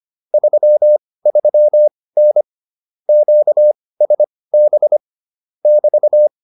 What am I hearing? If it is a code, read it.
Morse code, heard as a keyed tone, about 25 wpm, spaced as for 18 wpm, 600 Hz. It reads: 33N QSB <BT>